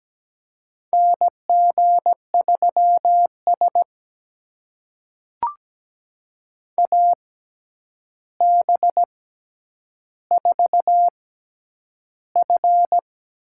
Morse code, heard as a keyed tone, about 17 wpm, spaced as for 9 wpm, 700 Hz. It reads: NG3S AB4F
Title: USA Call Signs - 4 Characters - 6x Character Spacing - 17wpm